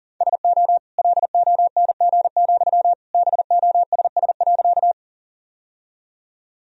Morse code, heard as a keyed tone, about 37 wpm, 690 Hz. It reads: SO LONG, BOSS.
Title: Sentences from Top 1000 Words - Review Part 4 - 40wpm